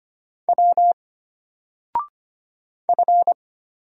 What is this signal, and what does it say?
Morse code, heard as a keyed tone, about 25 wpm, 700 Hz. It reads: W F
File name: Single-Letter-Number-25wpm